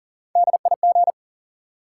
Morse code, heard as a keyed tone, about 40 wpm, 700 Hz. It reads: DIG
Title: Sentences from Top 1600 Words - New Word Drill - 40wpm